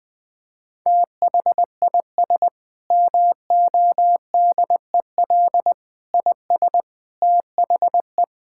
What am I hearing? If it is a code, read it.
Morse code, heard as a keyed tone, about 21 wpm, 700 Hz. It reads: THIS MODEL IS THE
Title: Sentences with 8-Word Phrases - Spoken Beginning - 20wpm